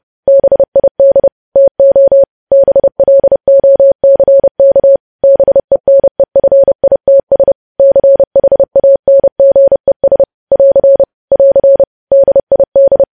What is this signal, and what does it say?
Morse code, heard as a keyed tone, about 30 wpm, 550 Hz. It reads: BID TO BLOCK BENEFITS CHANGES <AR> <AR> DID
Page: news headlines in morse code at 30 wpm